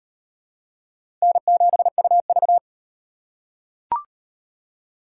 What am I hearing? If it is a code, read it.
Morse code, heard as a keyed tone, about 38 wpm, 700 Hz. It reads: N7UV